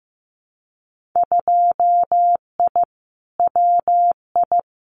Morse code, heard as a keyed tone, about 15 wpm, 700 Hz. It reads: 2I WI
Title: Winter Field Day Exchanges 15wpm